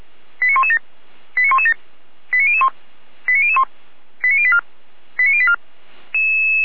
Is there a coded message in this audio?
Yes: five-tone selective call (ZVEI) from 0.4 to 5.6 s, heard as quick runs of five tones, tones of about 70 ms. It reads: 89117 89117 89001 89001 89084 89084